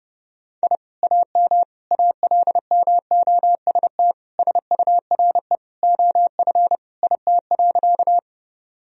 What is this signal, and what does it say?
Morse code, heard as a keyed tone, about 32 wpm, 700 Hz. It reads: I AM ALMOST SURE OF IT.
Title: Sentences from Top 1000 Words - Review Part 4 - 30wpm